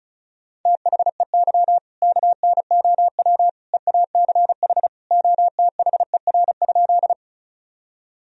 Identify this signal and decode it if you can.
Morse code, heard as a keyed tone, about 34 wpm, 700 Hz. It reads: THEY KNOW EACH OTHER?
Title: Sentences from Top 1000 Words - Review Part 1 - 35wpm